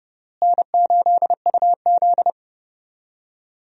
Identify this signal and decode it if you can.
Morse code, heard as a keyed tone, about 30 wpm, 700 Hz. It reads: N8UZ
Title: USA Call Signs - 4 Characters - Repeated 3x - 30wpm